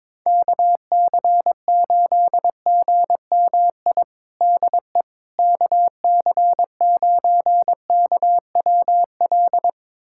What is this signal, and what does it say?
Morse code, heard as a keyed tone, about 22 wpm, 700 Hz. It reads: KC8GMI DE KC9KWL
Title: USA Callsign DE Callsign - 6 Characters - 22wpm